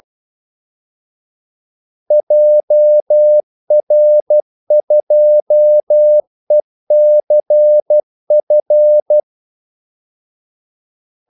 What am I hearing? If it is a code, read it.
Morse code, heard as a keyed tone, about 12 wpm, 600 Hz. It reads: JR2ECF
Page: 2 ekstra ordmellemrum mellem kaldesignalerne